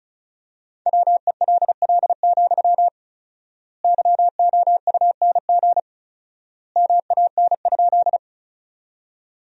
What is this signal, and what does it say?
Morse code, heard as a keyed tone, about 36 wpm, 700 Hz. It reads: WELL, YOUNG MAN?
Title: 4x Word-Spacing - Sentences from Top 400 Words - 35wpm